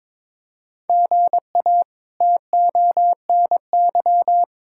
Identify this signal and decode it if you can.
Morse code, heard as a keyed tone, about 22 wpm, 700 Hz. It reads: GA TONY
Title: K1USN SST QSO Practice - 22wpm